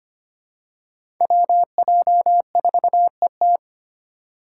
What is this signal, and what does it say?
Morse code, heard as a keyed tone, about 25 wpm, 700 Hz. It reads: WJ4ET
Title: USA Call Signs - 2x Character Spacing - 25wpm